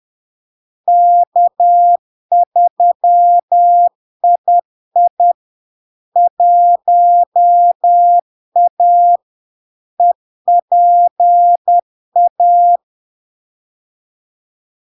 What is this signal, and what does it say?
Morse code, heard as a keyed tone, about 10 wpm, 700 Hz. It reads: K3II 1A EPA
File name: LARK-FD-2022-Log-in-CW-10wpm-700-Hz